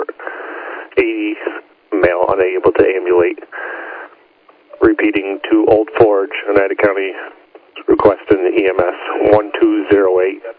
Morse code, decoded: none

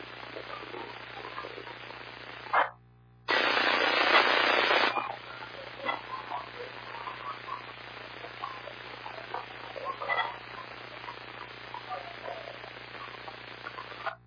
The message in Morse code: none

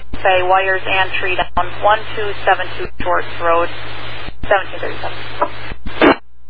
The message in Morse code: none

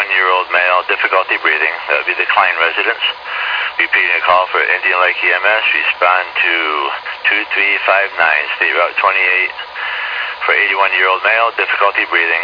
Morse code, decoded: none

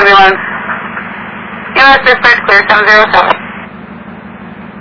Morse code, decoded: none

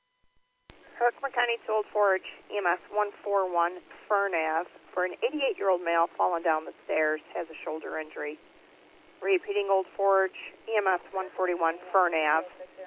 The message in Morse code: none

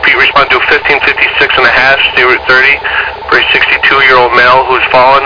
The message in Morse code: none

